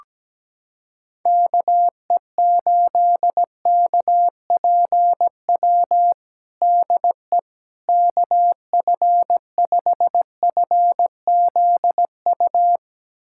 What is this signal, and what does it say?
Morse code, heard as a keyed tone, about 17 wpm, 700 Hz. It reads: KE8KPW DE KF5FZU
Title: USA Callsign DE Callsign 17wpm